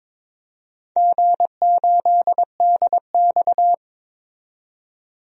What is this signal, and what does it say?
Morse code, heard as a keyed tone, about 23 wpm, 700 Hz. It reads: G8DX
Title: DXCC-100 Call Signs - 4 Characters - 22wpm